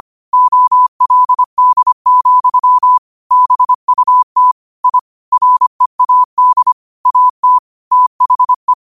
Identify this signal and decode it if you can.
Morse code, heard as a keyed tone, about 25 wpm, 1000 Hz. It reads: OLD, BUT I READ AT THE
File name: qotdfiles / 25wpm